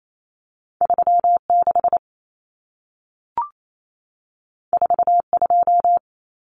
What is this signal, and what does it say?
Morse code, heard as a keyed tone, about 28 wpm, 700 Hz. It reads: 36 42
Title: Common Cut Numbers - 2-Digit 28wpm